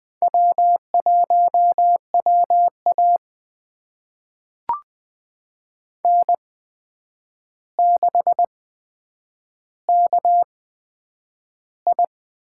Morse code, heard as a keyed tone, about 21 wpm, spaced as for 9 wpm, 700 Hz. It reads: W1WA N6KI
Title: USA Call Signs - 4 Characters - 8x Character Spacing - 20wpm